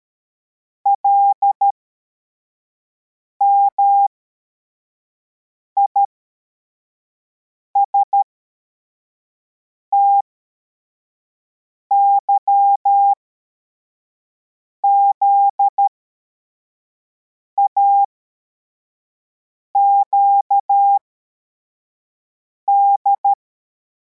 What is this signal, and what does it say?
Morse code, heard as a keyed tone, about 13 wpm, spaced as for 4 wpm, 800 Hz. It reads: LMISTYZAQD